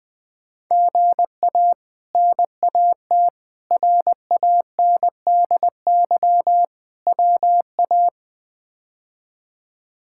Morse code, heard as a keyed tone, about 20 wpm, 700 Hz. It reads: GA NAT RANDY WA
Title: K1USN SST QSO Practice - 20wpm